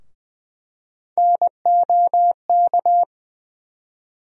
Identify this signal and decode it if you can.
Morse code, heard as a keyed tone, about 21 wpm, 700 Hz. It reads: NOK
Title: USA Call Sign Postfixes - 3 Letters 20wpm